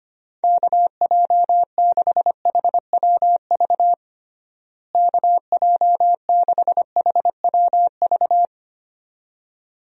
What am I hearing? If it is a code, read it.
Morse code, heard as a keyed tone, about 25 wpm, 700 Hz. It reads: KJ6HWV KJ6HWV
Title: USA Call Signs - 6 Characters - Repeated 2x - 25wpm